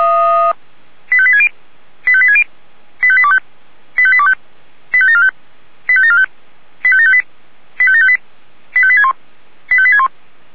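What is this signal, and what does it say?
Five-tone selective call (ZVEI), heard as quick runs of five tones, tones of about 70 ms. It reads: 85860 85860 85825 85825 85745 85745 85758 85758 85861 85861
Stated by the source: ZVEI five-tone selective call received on an SDR.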